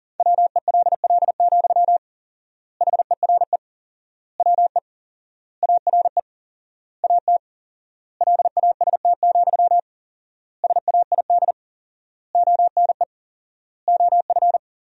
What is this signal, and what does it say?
Morse code, heard as a keyed tone, about 40 wpm, 700 Hz. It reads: WELL, HERE WE ARE AT LAST, SAID ONE OF
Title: 4x Word-Spacing - Sentences from Top 300 Words - 40wpm